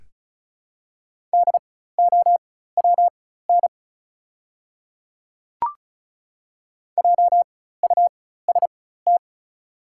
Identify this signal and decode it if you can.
Morse code, heard as a keyed tone, about 36 wpm, spaced as for 16 wpm, 700 Hz. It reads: DOWN JUST
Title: 4x Character Spacing - Top 200 Words - 35wpm